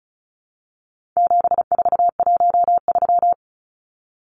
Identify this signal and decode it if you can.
Morse code, heard as a keyed tone, about 34 wpm, 700 Hz. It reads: 7413
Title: Numbers - 4-Digit 35wpm